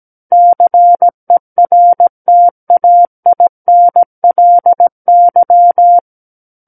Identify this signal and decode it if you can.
Morse code, heard as a keyed tone, about 17 wpm, 700 Hz. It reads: CERTAINLY
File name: Book One / 08-Chapter 8 - 17 WPM